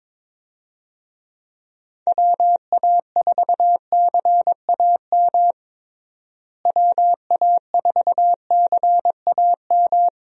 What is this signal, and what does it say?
Morse code, heard as a keyed tone, about 22 wpm, 700 Hz. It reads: WA4CAM WA4CAM
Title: USA Call Signs - 6 Characters - Repeated 2x - 22wpm